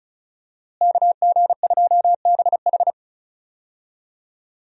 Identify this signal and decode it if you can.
Morse code, heard as a keyed tone, about 36 wpm, 700 Hz. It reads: KG2BH
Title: USA Call Signs - 5 Characters - Repeated 3x - 35wpm